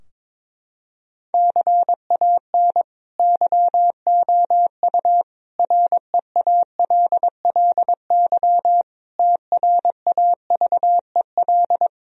Morse code, heard as a keyed tone, about 22 wpm, 700 Hz. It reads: CAN YOU REALLY TRAVEL
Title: Sentences from Top 1000 Words - Review Part 2 - 22wpm